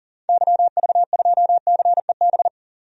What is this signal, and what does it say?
Morse code, heard as a keyed tone, about 40 wpm, 700 Hz. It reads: YV2CEB
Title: DXCC-100 Call Signs - 6 Characters - 40wpm